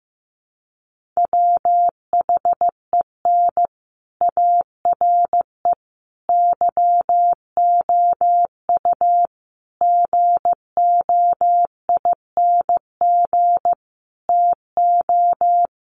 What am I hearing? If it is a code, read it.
Morse code, heard as a keyed tone, about 15 wpm, 700 Hz. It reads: WHEN ARE YOU GOING TO
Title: Sets of 5 Words Encore2 15wpm